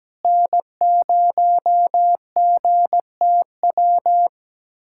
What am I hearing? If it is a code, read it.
Morse code, heard as a keyed tone, about 17 wpm, 700 Hz. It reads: N0GTW
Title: USA Callsign DE Callsign - 5 Characters - 17wpm